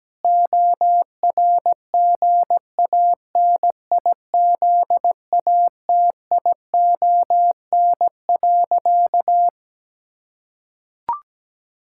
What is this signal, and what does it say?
Morse code, heard as a keyed tone, about 17 wpm, 700 Hz. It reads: ORGANIZATION.
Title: Sentences from Top 1000 Words - Review Part 6 - 17wpm